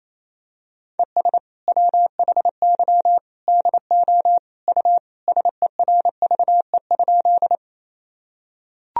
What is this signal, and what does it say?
Morse code, heard as a keyed tone, about 28 wpm, 700 Hz. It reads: ES WHY DO U SERVE?
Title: Sentences from Top 400 Words and Abbreviations 28wpm